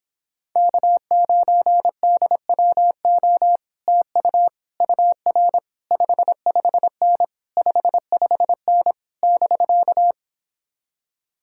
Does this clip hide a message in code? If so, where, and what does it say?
Yes: Morse code from 0.6 to 10.1 s, heard as a keyed tone, about 26 wpm, 700 Hz. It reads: K9DWO TU UR 55N 55N <BK>